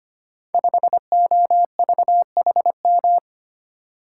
Morse code, heard as a keyed tone, about 25 wpm, 700 Hz. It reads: 5OVHM
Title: Code Groups - 5 Letter:Numbers - 25wpm